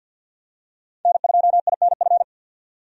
Morse code, heard as a keyed tone, about 52 wpm, 700 Hz. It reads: N2INF
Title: USA Call Signs - 5 Characters - 50wpm